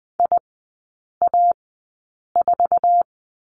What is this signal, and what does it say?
Morse code, heard as a keyed tone, about 20 wpm, spaced as for 8 wpm, 700 Hz. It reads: IA4
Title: 133 Lesson 22e - Question Mark - Three Characters 20wpm